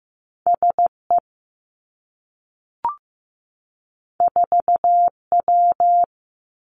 Morse code, heard as a keyed tone, about 15 wpm, 700 Hz. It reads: S E 4 W
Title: Two Challenging Letter-Number Combinations 15wpm